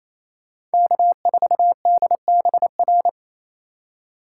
Morse code, heard as a keyed tone, about 33 wpm, 700 Hz. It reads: K4DBR
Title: Speed Racing - USA 1x3 Call Signs 28wpm